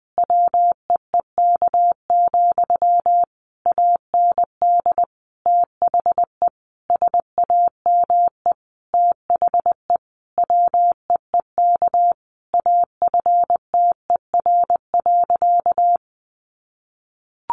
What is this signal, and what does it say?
Morse code, heard as a keyed tone, about 20 wpm, 700 Hz. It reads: WEEK, AND THE SAME THE WEEK AFTER.
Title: Sentences from Top 500 Words 20wpm